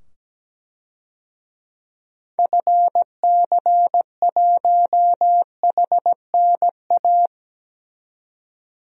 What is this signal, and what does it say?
Morse code, heard as a keyed tone, about 17 wpm, 700 Hz. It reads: FC1HNA
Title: DXCC-100 Call Signs - 6 Characters - 17wpm